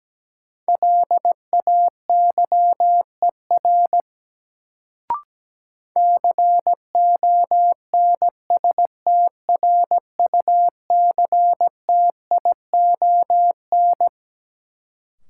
Morse code, heard as a keyed tone, about 17 wpm, 700 Hz. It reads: LAYER CONSTRUCTION
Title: Sentences from Top 1600 Words - New Word Drill - 17wpm